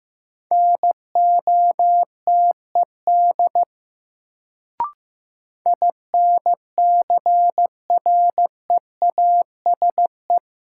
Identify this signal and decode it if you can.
Morse code, heard as a keyed tone, about 15 wpm, 700 Hz. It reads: NOTED INCREASE
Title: Sets of 5 Words - New Word Drill - 15wpm